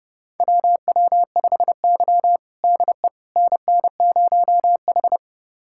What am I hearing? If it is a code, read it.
Morse code, heard as a keyed tone, about 30 wpm, 700 Hz. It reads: WW5Y DE NN0H
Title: USA Callsign DE Callsign - 4 Characters - 30wpm